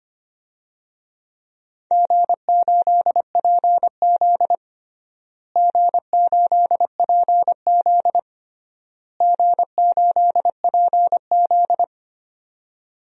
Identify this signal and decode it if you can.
Morse code, heard as a keyed tone, about 25 wpm, 700 Hz. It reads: G8PZ G8PZ G8PZ
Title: DXCC-100 Call Signs - 4 Characters - Repeated 3x - 25wpm